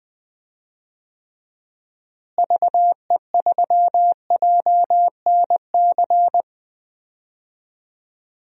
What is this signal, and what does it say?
Morse code, heard as a keyed tone, about 21 wpm, 700 Hz. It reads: VE3JNC
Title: DXCC-100 Call Signs - 6 Characters - Repeated 2x - 20wpm